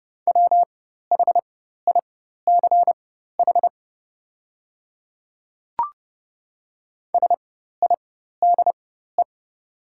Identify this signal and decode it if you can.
Morse code, heard as a keyed tone, about 30 wpm, spaced as for 14 wpm, 700 Hz. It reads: WHICH SIDE
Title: 4x Character Spacing - Top 100 Words - 30wpm